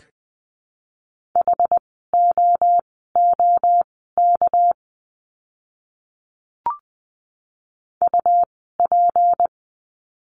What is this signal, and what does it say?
Morse code, heard as a keyed tone, about 20 wpm, spaced as for 14 wpm, 700 Hz. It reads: HOOK UP
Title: 199 Lesson 32f - Letter k - Words 20wpm